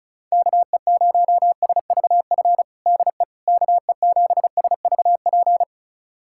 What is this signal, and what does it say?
Morse code, heard as a keyed tone, about 34 wpm, 700 Hz. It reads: KE0SVF DE KE7SVP
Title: USA Callsign DE Callsign - 6 Characters - 35wpm